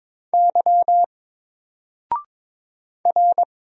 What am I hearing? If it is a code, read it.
Morse code, heard as a keyed tone, about 22 wpm, 700 Hz. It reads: Y R
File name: Single-Letter-Number-22wpm